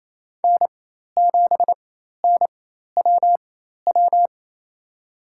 Morse code, heard as a keyed tone, about 28 wpm, spaced as for 13 wpm, 700 Hz. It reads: N7NWW
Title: USA Call Signs - 5 Characters - 4x Character Spacing - 28wpm